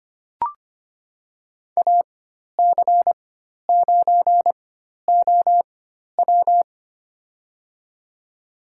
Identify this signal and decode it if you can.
Morse code, heard as a keyed tone, about 25 wpm, spaced as for 12 wpm, 700 Hz. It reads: AC9OW
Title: USA Call Signs - 5 Characters - 4x Character Spacing - 25wpm